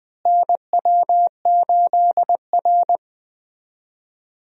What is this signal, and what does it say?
Morse code, heard as a keyed tone, about 20 wpm, 700 Hz. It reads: NW8R
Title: USA Call Signs - 4 Characters - Repeated 2x - 20wpm